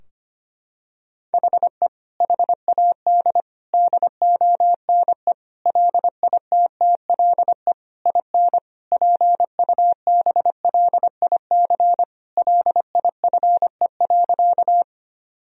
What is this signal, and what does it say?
Morse code, heard as a keyed tone, about 25 wpm, 700 Hz. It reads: HE HAD DONE LITTLE IN PUBLIC LIFE.
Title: Sentences from Top 1000 Words - Review Part 3 - 25wpm